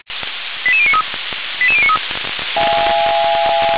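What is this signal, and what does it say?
Five-tone selective call (ZVEI), heard as quick runs of five tones, tones of about 70 ms. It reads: 88093 88093
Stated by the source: ZVEI five-tone selective call received on an SDR.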